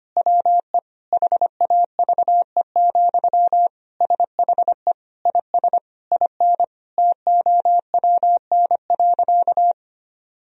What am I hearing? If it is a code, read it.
Morse code, heard as a keyed tone, about 25 wpm, 700 Hz. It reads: WE HAVE, SHE IS IN TOWN.
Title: Sentences from Top 1000 Words - Review Part 4 - 25wpm